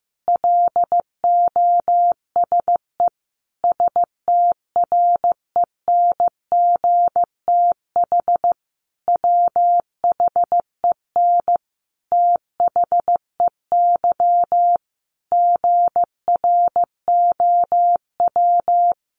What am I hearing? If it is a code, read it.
Morse code, heard as a keyed tone, about 14 wpm, 700 Hz. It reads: LOSE STRENGTH WHEN THEY GROW
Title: Sentences from Top 1300 Words 15wpm